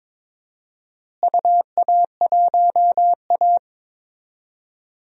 Morse code, heard as a keyed tone, about 22 wpm, 700 Hz. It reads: UA1A
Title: DXCC-100 Call Signs - 4 Characters - Repeated 3x - 22wpm